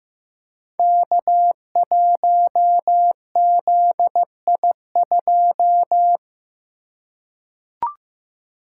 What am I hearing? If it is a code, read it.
Morse code, heard as a keyed tone, about 15 wpm, 700 Hz. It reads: K1ZI2
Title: Code Groups - 5 Letter:Numbers - 15wpm